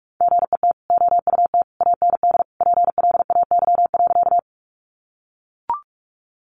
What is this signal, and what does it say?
Morse code, heard as a keyed tone, about 44 wpm, 700 Hz. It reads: GET OUT AND PLAY.
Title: Sentences from Top 200 Words 45wpm